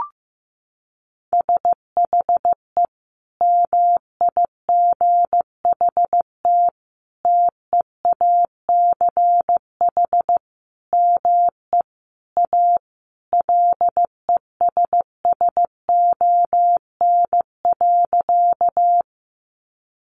Morse code, heard as a keyed tone, about 15 wpm, 700 Hz. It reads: SHE MIGHT TEACH ME A LESSON.
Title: Sentences from Top 1200 Words 15wpm